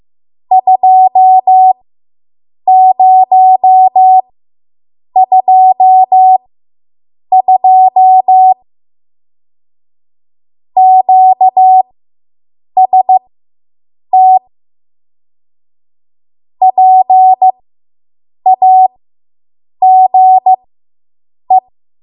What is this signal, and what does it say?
Morse code, heard as a keyed tone, about 15 wpm, spaced as for 7 wpm, 750 Hz. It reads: 2022 QST PAGE